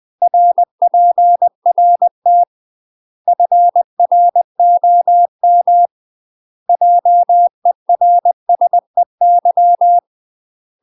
Morse code, heard as a keyed tone, about 20 wpm, 700 Hz. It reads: RPRT FROM JERSEY
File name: UK MASTER QSOs TRAINING / 20WPM / UK MASTER QSOs 0071